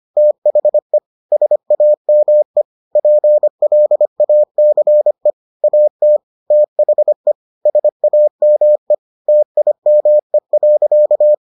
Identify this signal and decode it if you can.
Morse code, heard as a keyed tone, about 25 wpm, 600 Hz. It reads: THE SAME PLACE AT THE SAME TIME.